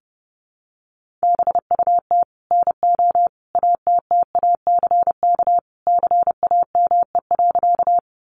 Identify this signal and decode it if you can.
Morse code, heard as a keyed tone, about 30 wpm, 700 Hz. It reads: BUT NO ATTACK CAME.
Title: Sentences from Top 1500 Words 30wpm